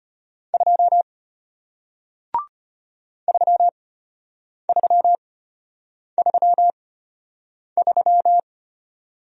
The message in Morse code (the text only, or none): 2 3333